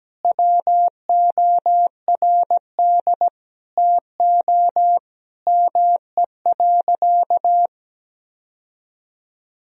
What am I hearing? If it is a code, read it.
Morse code, heard as a keyed tone, about 17 wpm, 700 Hz. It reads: WORD TO ME.
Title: Sentences from Top 1000 Words - Review Part 4 - 17wpm